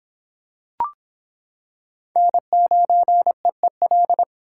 Morse code, heard as a keyed tone, about 26 wpm, 700 Hz. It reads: N9EEL